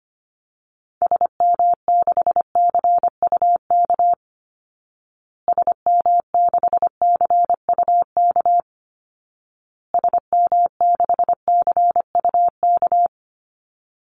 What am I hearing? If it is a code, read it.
Morse code, heard as a keyed tone, about 25 wpm, 700 Hz. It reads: SM6CUK SM6CUK SM6CUK
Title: Call Signs Repeated 2x - 25wpm